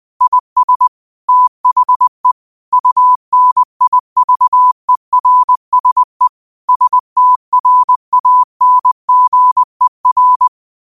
Morse code, heard as a keyed tone, about 20 wpm, 1000 Hz. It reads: IS THE UNIVERSE STRANGER
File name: qotdfiles / 20wpm